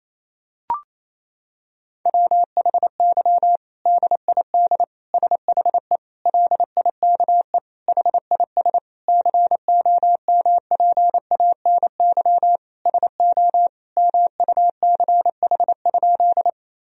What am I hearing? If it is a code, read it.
Morse code, heard as a keyed tone, about 27 wpm, 700 Hz. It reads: WHY DID SHE LIKE HIS COMPANY SO MUCH?
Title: Sentences from Top 1000 Words - Review Part 3 - 28wpm